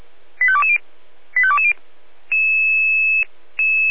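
Five-tone selective call (ZVEI), heard as quick runs of five tones, tones of about 70 ms. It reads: 85220 85220
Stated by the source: Five-tone ZVEI selective call 85242, received on SDR.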